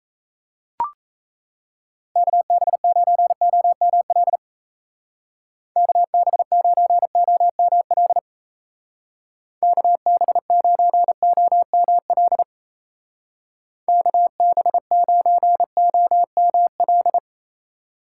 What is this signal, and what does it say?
Morse code, heard as a keyed tone, about 34 wpm, 700 Hz. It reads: KB9OML KB9OML KB9OML KB9OML
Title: Speed Racing - USA 2x3 Call Signs 28wpm